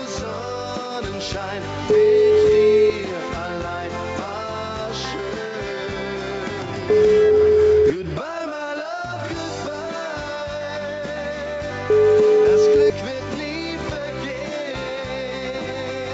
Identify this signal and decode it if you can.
Ringback tone (Europe, 425 Hz), with noise.